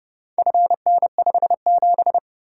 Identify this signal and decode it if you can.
Morse code, heard as a keyed tone, about 29 wpm, 700 Hz. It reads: FN57